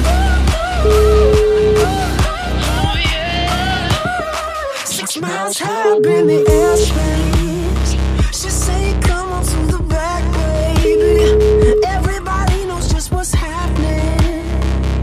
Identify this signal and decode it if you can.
Ringback tone (Europe, 425 Hz), with noise.